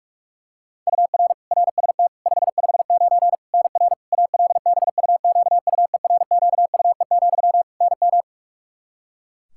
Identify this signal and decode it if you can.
Morse code, heard as a keyed tone, about 41 wpm, 700 Hz. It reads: UR RST 559 NR ALBUQUERQUE, NM
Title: Practice QSO - Exchange 1 - Beginning - 45wpm